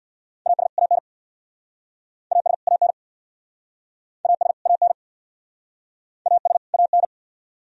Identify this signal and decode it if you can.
Morse code, heard as a keyed tone, about 67 wpm, 700 Hz. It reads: AS AN AS AN AS AN AS AN